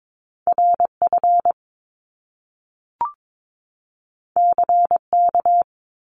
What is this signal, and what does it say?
Morse code, heard as a keyed tone, about 22 wpm, 700 Hz. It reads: RF CK
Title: Top Two Letter Combinations in Words 22wpm